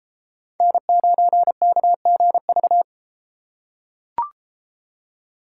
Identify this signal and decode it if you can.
Morse code, heard as a keyed tone, about 30 wpm, 700 Hz. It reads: N9KGV